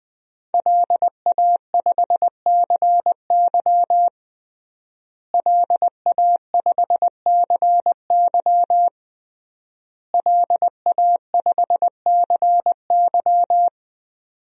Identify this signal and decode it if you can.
Morse code, heard as a keyed tone, about 20 wpm, 700 Hz. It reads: LA5CY LA5CY LA5CY
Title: DXCC-100 Call Signs - 5 Characters - Repeated 3x - 20wpm